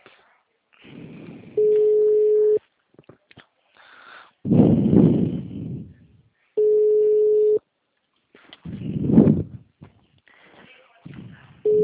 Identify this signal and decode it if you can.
Ringback tone (Europe, 425 Hz).
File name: out- -2004- - - .2